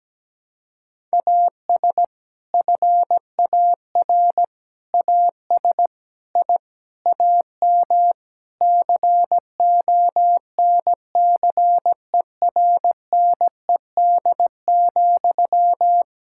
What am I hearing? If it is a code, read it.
Morse code, heard as a keyed tone, about 17 wpm, 700 Hz. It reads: AS FAR AS I AM CONCERNED,
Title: Sentences from Top 1000 Words - Review Part 6 - 17wpm